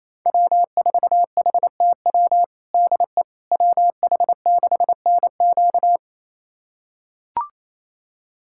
Morse code, heard as a keyed tone, about 28 wpm, 700 Hz. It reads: W4HTW DE WH6NQ
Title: USA Callsign DE Callsign - 5 Characters - 28wpm